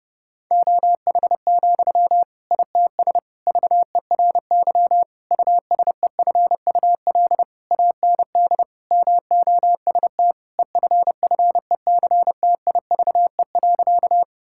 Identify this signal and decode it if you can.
Morse code, heard as a keyed tone, about 30 wpm, 700 Hz. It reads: OH, ITS VERY USEFUL AND MOST EFFECTIVE.
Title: Sentences from Top 1000 Words - Review Part 6 - 30wpm